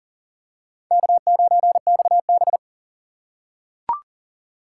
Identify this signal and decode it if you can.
Morse code, heard as a keyed tone, about 40 wpm, 700 Hz. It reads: K9XB
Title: USA Call Signs - 4 Characters - 4x Character Spacing - 40wpm